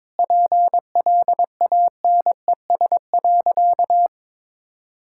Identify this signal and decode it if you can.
Morse code, heard as a keyed tone, about 22 wpm, 700 Hz. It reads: PLANES.
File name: practice / 22wpm / Easy Sentences with Hints - 22wpm